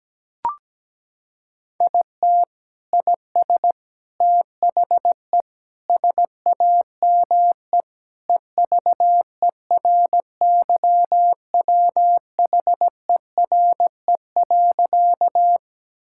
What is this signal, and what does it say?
Morse code, heard as a keyed tone, about 17 wpm, 700 Hz. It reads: IT IS THE SAME EVERYWHERE.
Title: Sentences from Top 1600 Words 17wpm